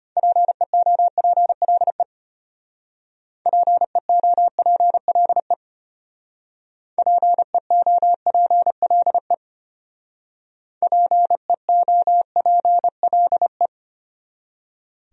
Morse code, heard as a keyed tone, about 32 wpm, 700 Hz. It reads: PEOPLE PEOPLE PEOPLE PEOPLE